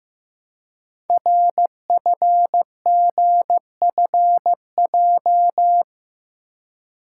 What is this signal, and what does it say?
Morse code, heard as a keyed tone, about 15 wpm, 700 Hz. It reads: RFGFJ
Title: Code Groups - 5 Letters - 15wpm